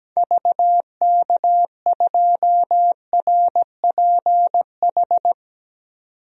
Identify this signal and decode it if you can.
Morse code, heard as a keyed tone, about 17 wpm, 700 Hz. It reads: VK2RPH
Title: DXCC-100 Call Signs - 6 Characters - 17wpm